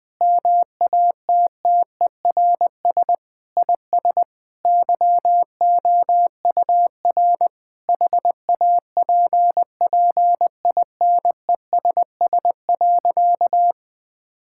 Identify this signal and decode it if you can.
Morse code, heard as a keyed tone, about 21 wpm, 700 Hz. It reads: MATTERS IS YOUR HAPPINESS.
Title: Sentences with 6-Word Phrases - 20wpm